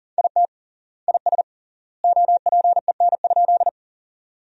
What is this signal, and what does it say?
Morse code, heard as a keyed tone, about 40 wpm, 700 Hz. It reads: IT IS OPEN?
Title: 3x Word-Spacing - Sentences from Top 300 Words - 40wpm